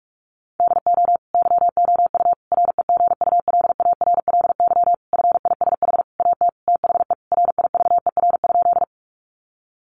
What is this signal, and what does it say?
Morse code, heard as a keyed tone, about 44 wpm, 700 Hz. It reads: DO YOU REGULARLY FISH AT THE RIVER?
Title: Sentences from Top 1400 Words 45wpm